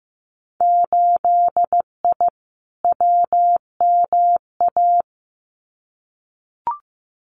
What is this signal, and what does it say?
Morse code, heard as a keyed tone, about 15 wpm, 700 Hz. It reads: 8I WMA
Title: Winter Field Day Exchanges Encore 15wpm